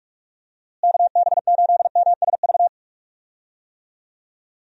Morse code, heard as a keyed tone, about 46 wpm, 700 Hz. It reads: KB8MSV
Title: USA Call Signs - 6 Characters - Repeated 2x - 45wpm